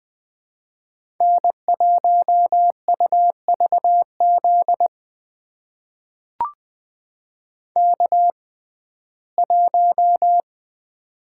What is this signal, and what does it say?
Morse code, heard as a keyed tone, about 20 wpm, spaced as for 14 wpm, 700 Hz. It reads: N1UVZ K1
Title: USA Call Signs - 5 Characters - 6x Character Spacing - 20wpm